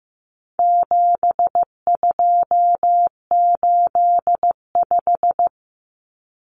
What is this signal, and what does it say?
Morse code, heard as a keyed tone, about 15 wpm, 700 Hz. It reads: 7285
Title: Common Cut Numbers - 4-Digit 15wpm